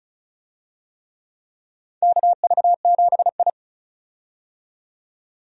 Morse code, heard as a keyed tone, about 34 wpm, 700 Hz. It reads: KV7I
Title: USA Call Signs - 4 Characters - 35wpm